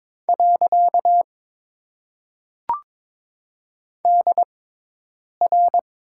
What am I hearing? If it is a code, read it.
Morse code, heard as a keyed tone, about 22 wpm, spaced as for 8 wpm, 700 Hz. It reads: . DR
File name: Top 90 QSO Elements - 6x Character Spacing - 22wpm